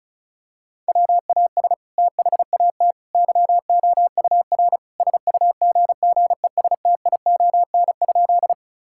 Morse code, heard as a keyed tone, about 36 wpm, 700 Hz. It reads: WAS THAT YOUR SUGGESTION?
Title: Sentences from Top 1600 Words 35wpm